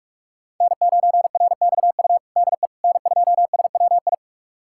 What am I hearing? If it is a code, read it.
Morse code, heard as a keyed tone, about 42 wpm, 700 Hz. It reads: N9RXU DE N2SWI
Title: USA Callsign DE Callsign - 5 Characters - 45wpm